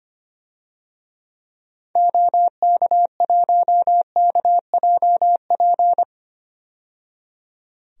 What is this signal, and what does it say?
Morse code, heard as a keyed tone, about 25 wpm, 700 Hz. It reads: OK1KJP
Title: DXCC-100 Call Signs - 6 Characters - 25wpm